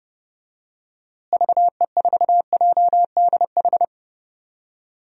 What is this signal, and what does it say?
Morse code, heard as a keyed tone, about 30 wpm, 700 Hz. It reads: VE4JDH
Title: DXCC-100 Call Signs - 6 Characters - Repeated 2x - 30wpm